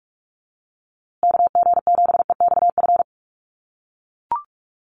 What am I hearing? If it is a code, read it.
Morse code, heard as a keyed tone, about 44 wpm, 700 Hz. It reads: KG7EXF
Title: USA Call Signs Repeated 3x 45wpm